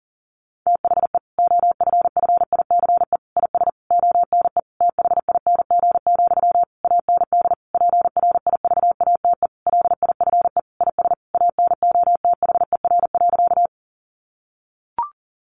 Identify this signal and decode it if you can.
Morse code, heard as a keyed tone, about 40 wpm, 700 Hz. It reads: THE OFFICE IS ONE THING, AND PRIVATE LIFE IS ANOTHER.
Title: Sentences from Top 1000 Words 40wpm